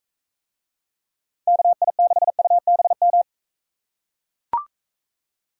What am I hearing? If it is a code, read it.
Morse code, heard as a keyed tone, about 42 wpm, 700 Hz. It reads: KI6UBM